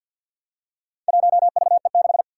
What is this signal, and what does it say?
Morse code, heard as a keyed tone, about 48 wpm, 700 Hz. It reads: 1VE6
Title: Four Challenging Letter-Number Combinations 50wpm